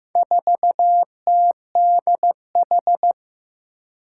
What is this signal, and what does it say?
Morse code, heard as a keyed tone, about 15 wpm, 700 Hz. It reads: 4TDH
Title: Four Challenging Letter-Number Combinations 15wpm